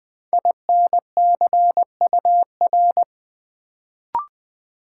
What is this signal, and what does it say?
Morse code, heard as a keyed tone, about 20 wpm, 700 Hz. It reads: INCUR
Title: Build A Word - 1 Letter At A Time - 3 to 5-Letter Words - 20wpm